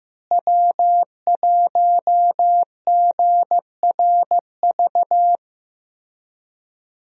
Morse code, heard as a keyed tone, about 15 wpm, 700 Hz. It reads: W1GRV
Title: USA Call Signs - 5 Characters - Repeated 2x - 15wpm